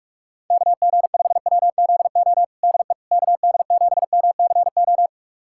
Morse code, heard as a keyed tone, about 46 wpm, 700 Hz. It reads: KG5WZO DE KD7MCO
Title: USA Callsign DE Callsign 45wpm